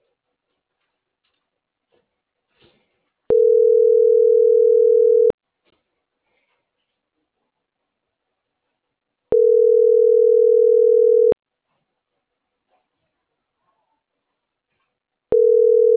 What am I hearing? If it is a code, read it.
Ringback tone (North America), 440 + 480 Hz.